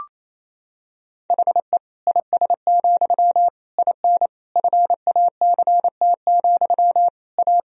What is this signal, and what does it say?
Morse code, heard as a keyed tone, about 27 wpm, 700 Hz. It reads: HE IS, IN FACT, A
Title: Sentences from Top 1000 Words - Review Part 3 - 28wpm